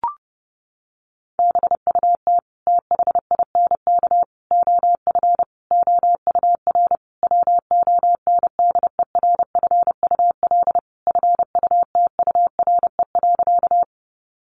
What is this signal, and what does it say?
Morse code, heard as a keyed tone, about 30 wpm, 700 Hz. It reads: BUT THINK OF OUR WONDERFUL FUTURE.
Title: Sentences from Top 1100 Words 30wpm